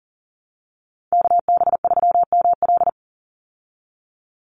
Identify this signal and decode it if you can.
Morse code, heard as a keyed tone, about 39 wpm, 700 Hz. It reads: KB3ML
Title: Call Signs Repeated 3x - 40wpm